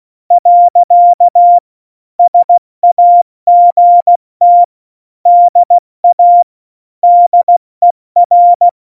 Morse code, heard as a keyed tone, about 16 wpm, 700 Hz. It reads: . SAGT DA DER
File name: Morsekurs / Witze / Witz 14. Januar 2020, 70 F80